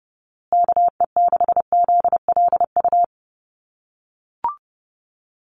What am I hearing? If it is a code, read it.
Morse code, heard as a keyed tone, about 30 wpm, 700 Hz. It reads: KE6ZLU
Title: USA Call Signs Repeated 2x 30wpm